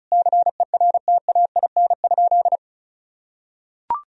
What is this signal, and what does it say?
Morse code, heard as a keyed tone, about 34 wpm, 700 Hz. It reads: CERTAIN?
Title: Sentences from Top 1000 Words - Review Part 2 - 35wpm